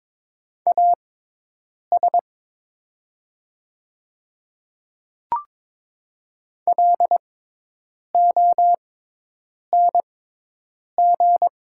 Morse code, heard as a keyed tone, about 22 wpm, spaced as for 8 wpm, 700 Hz. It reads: AS LONG
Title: 6x Character Spacing - Top 100 Words - 22wpm